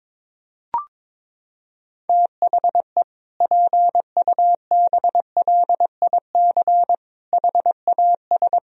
Morse code, heard as a keyed tone, about 22 wpm, 700 Hz. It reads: THE PUBLIC HAS
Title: Sentences from Top 1000 Words - Review Part 3 - 22wpm